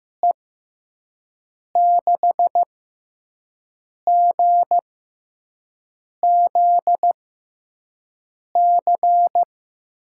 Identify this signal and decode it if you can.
Morse code, heard as a keyed tone, about 15 wpm, spaced as for 5 wpm, 700 Hz. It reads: E6GZC
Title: USA Call Signs - 6 Characters - 6x Character Spacing - 15wpm